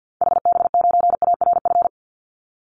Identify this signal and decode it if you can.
Morse code, heard as a keyed tone, about 50 wpm, 700 Hz. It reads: HB9ARF
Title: Call Signs - 50wpm